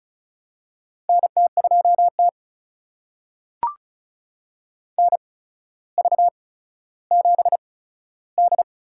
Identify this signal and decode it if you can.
Morse code, heard as a keyed tone, about 36 wpm, spaced as for 15 wpm, 700 Hz. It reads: NT2T NV7D
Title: USA Call Signs - 4 Characters - 8x Character Spacing - 35wpm